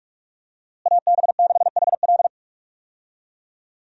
Morse code, heard as a keyed tone, about 41 wpm, 700 Hz. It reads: AB6HL
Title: USA Call Signs - 5 Characters - Repeated 3x - 45wpm